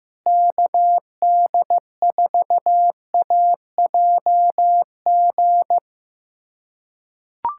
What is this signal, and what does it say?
Morse code, heard as a keyed tone, about 16 wpm, 700 Hz. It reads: KD4AJG
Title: USA Callsign DE Callsign - 6 Characters - 15wpm